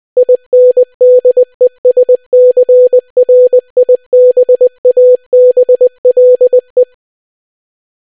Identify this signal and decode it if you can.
Morse code, heard as a keyed tone, about 20 wpm, 500 Hz. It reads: INDESCRIBABLE